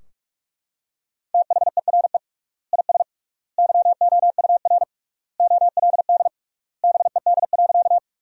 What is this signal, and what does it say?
Morse code, heard as a keyed tone, about 42 wpm, 700 Hz. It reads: THERE IS YOUR OLD BED.
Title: 3x Word-Spacing - Sentences from Top 500 Words - 45wpm